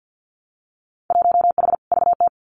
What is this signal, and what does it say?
Morse code, heard as a keyed tone, about 50 wpm, 700 Hz. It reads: 1H VT
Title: Winter Field Day Exchanges Encore 50wpm